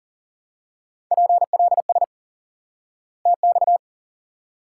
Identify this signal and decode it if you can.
Morse code, heard as a keyed tone, about 40 wpm, 700 Hz. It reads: PLS TX
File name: IQE Course 3 - Lesson 89 - YAGI - Two Elements - 40wpm